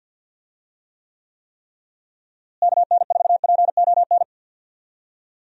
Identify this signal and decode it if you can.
Morse code, heard as a keyed tone, about 51 wpm, 700 Hz. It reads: KN4PON
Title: USA Call Signs - 6 Characters - Repeated 3x - 50wpm